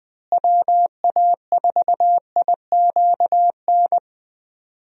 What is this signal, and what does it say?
Morse code, heard as a keyed tone, about 20 wpm, 700 Hz. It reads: WA4IQN
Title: USA Call Signs - 6 Characters - Repeated 3x - 20wpm